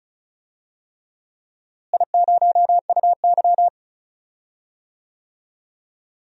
Morse code, heard as a keyed tone, about 34 wpm, 700 Hz. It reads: I0UY
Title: DXCC-100 Call Signs - 4 Characters - Repeated 3x - 35wpm